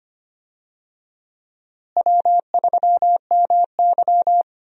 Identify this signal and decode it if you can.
Morse code, heard as a keyed tone, about 25 wpm, 700 Hz. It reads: W3MY
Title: USA Call Signs - 4 Characters - Repeated 3x - 25wpm